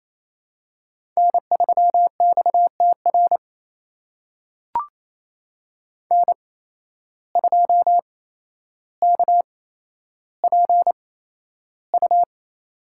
Morse code, heard as a keyed tone, about 28 wpm, spaced as for 12 wpm, 700 Hz. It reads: N3XTR N2KPU
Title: USA Call Signs - 5 Characters - 8x Character Spacing - 28wpm